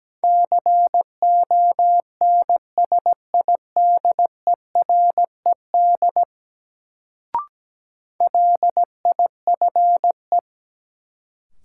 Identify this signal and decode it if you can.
Morse code, heard as a keyed tone, about 17 wpm, 700 Hz. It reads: CONSIDERED LIFE
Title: Sets of 5 Words - New Word Drill - 17wpm